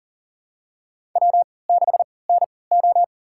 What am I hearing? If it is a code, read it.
Morse code, heard as a keyed tone, about 40 wpm, spaced as for 23 wpm, 700 Hz. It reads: W6NO
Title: USA Call Signs - 4 Characters - 3x Character Spacing - 40wpm